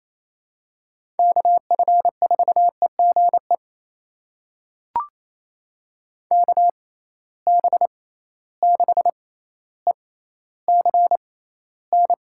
Morse code, heard as a keyed tone, about 27 wpm, spaced as for 14 wpm, 700 Hz. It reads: KF4EGE KB6ECN
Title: USA Call Signs - 6 Characters - 6x Character Spacing - 28wpm